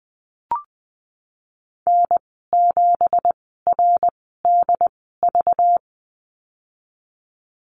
Morse code, heard as a keyed tone, about 20 wpm, spaced as for 15 wpm, 700 Hz. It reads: N7RDV
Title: 193 Lesson 31g - Letter v - Call Signs 20wpm